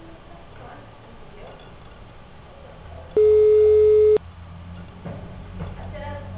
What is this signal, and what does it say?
Ringback tone (Europe, 425 Hz), with speech.
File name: out- -2011- - - .13463